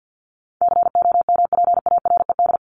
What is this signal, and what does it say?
Morse code, heard as a keyed tone, about 51 wpm, 700 Hz. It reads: COMPARED
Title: ED Suffix - Top 100 Words with Base Words 50wpm